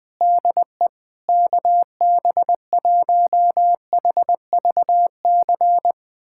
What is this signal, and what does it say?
Morse code, heard as a keyed tone, about 20 wpm, 700 Hz. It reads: DE KB1HVC
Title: USA Callsign DE Callsign 20wpm